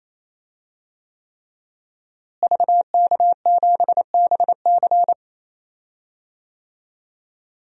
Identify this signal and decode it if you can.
Morse code, heard as a keyed tone, about 27 wpm, 700 Hz. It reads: VK7BC
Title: DXCC-100 Call Signs - 5 Characters - Repeated 3x - 28wpm